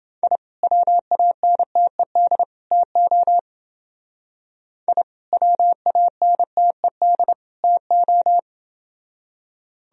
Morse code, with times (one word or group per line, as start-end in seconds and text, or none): 0.2-0.4 I
0.6-2.4 WANTED
2.7-3.4 TO
4.9-5.0 I
5.3-7.3 WANTED
7.6-8.4 TO